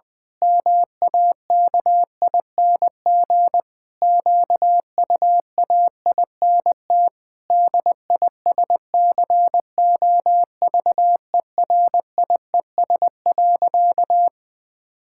Morse code, heard as a keyed tone, about 20 wpm, 700 Hz. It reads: MAKING QUAINT DISCOVERIES.
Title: Pangram Challenge v2 - 20wpm